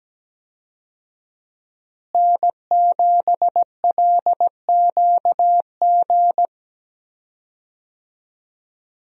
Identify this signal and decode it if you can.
Morse code, heard as a keyed tone, about 17 wpm, 700 Hz. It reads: N7LQG
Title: USA Call Signs - 5 Characters - Repeated 3x - 17wpm